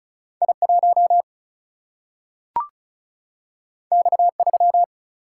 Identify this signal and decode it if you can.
Morse code, heard as a keyed tone, about 36 wpm, 700 Hz. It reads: I 1 X 3
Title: Single-Letter Single-Number 35wpm